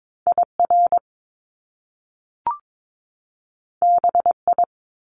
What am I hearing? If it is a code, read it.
Morse code, heard as a keyed tone, about 22 wpm, 700 Hz. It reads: IR BI
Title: Top Two Letter Combinations in Words 22wpm